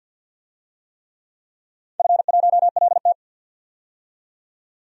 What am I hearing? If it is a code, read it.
Morse code, heard as a keyed tone, about 52 wpm, 700 Hz. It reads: F1LT
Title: DXCC-100 Call Signs - 4 Characters - 50wpm